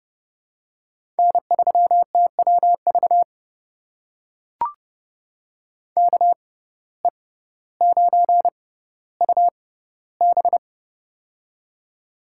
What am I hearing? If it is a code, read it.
Morse code, heard as a keyed tone, about 32 wpm, spaced as for 16 wpm, 700 Hz. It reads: N3TWV KE9UB
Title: USA Call Signs - 5 Characters - 6x Character Spacing - 30wpm